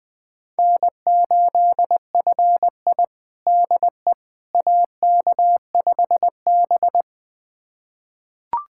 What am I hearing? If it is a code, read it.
Morse code, heard as a keyed tone, about 20 wpm, 700 Hz. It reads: N8FI DE AK5B
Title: USA Callsign DE Callsign - 4 Characters - 20wpm